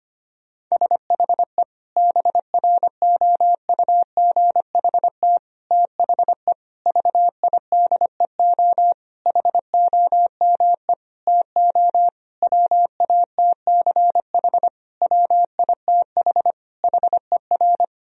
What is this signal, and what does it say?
Morse code, heard as a keyed tone, about 25 wpm, 700 Hz. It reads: SHE BROUGHT THE VIDEO HOME TO WATCH WITH HER
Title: Sentences from Top 1000 Words - Review Part 4 - 25wpm